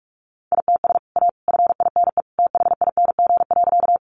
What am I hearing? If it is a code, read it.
Morse code, heard as a keyed tone, about 44 wpm, 700 Hz. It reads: ITS A FINE THING.
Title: Sentences from Top 600 Words 45wpm